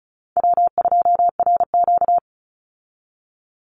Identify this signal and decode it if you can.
Morse code, heard as a keyed tone, about 34 wpm, 700 Hz. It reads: W2RQ
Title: Call Signs Repeated 3x - 35wpm